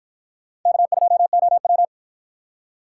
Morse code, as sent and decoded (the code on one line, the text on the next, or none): -.- ..--- --- .--
K2OW